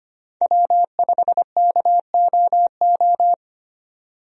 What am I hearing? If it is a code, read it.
Morse code, heard as a keyed tone, about 25 wpm, 700 Hz. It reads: W5KOO
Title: USA Call Signs - 5 Characters - 25wpm